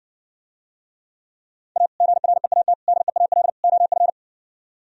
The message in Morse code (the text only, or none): A GREAT DEAL OF